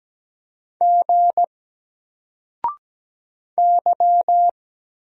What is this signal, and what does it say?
Morse code, heard as a keyed tone, about 17 wpm, 700 Hz. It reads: G Y
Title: Single Character (Letter:Number:Punctuation:Prosign) 17wpm